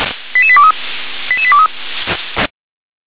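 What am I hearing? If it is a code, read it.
Five-tone selective call (ZVEI), heard as quick runs of five tones, tones of about 70 ms. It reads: 88823 88823
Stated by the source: ZVEI five-tone call received by SDR.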